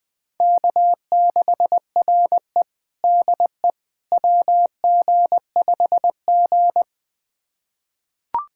Morse code, heard as a keyed tone, about 21 wpm, 700 Hz. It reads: K6RE DE WG5G
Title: USA Callsign DE Callsign - 4 Characters - 20wpm